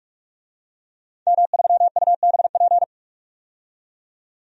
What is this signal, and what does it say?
Morse code, heard as a keyed tone, about 44 wpm, 700 Hz. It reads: M3UBP
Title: DXCC-100 Call Signs - 5 Characters - 45wpm